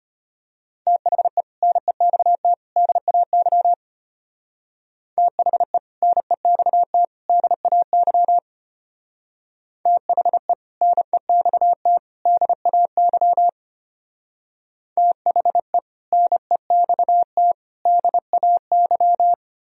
Morse code, heard as a keyed tone, about 31 wpm, 700 Hz. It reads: THE NEXT DAY THE NEXT DAY THE NEXT DAY THE NEXT DAY
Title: Speed-Racing - Sets of 3 Words 25wpm